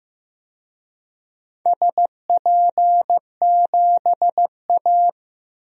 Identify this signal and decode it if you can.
Morse code, heard as a keyed tone, about 16 wpm, 700 Hz. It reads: SP7A
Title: DXCC-100 Call Signs - 4 Characters - Repeated 3x - 15wpm